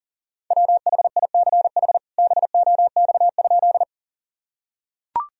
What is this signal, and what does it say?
Morse code, heard as a keyed tone, about 40 wpm, 700 Hz. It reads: WHICH BOX?
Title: Sentences from Top 1000 Words - Review Part 2 - 40wpm